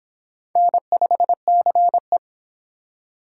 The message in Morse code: N5CE

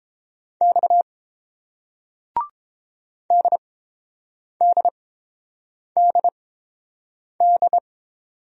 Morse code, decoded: X DDDD

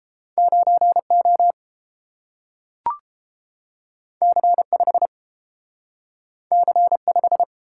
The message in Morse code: 9O C5 C5

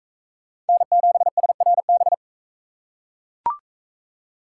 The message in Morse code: N7SRB

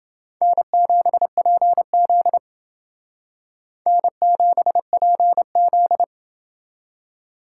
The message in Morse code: N7PZ N7PZ

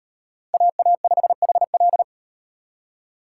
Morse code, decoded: AA5HL

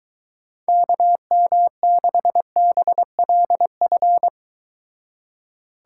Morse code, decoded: KM6BLF

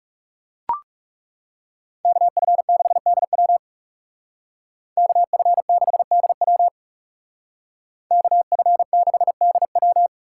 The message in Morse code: KF6DW KF6DW KF6DW

